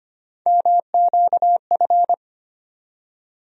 Morse code, heard as a keyed tone, about 25 wpm, 700 Hz. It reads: MQF
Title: USA Call Sign Postfixes - 3 Letters 25wpm